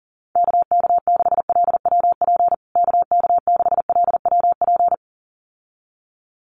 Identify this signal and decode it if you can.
Morse code, heard as a keyed tone, about 40 wpm, 700 Hz. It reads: KK6LWP KK6LWP
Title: USA Call Signs Repeated 2x 40wpm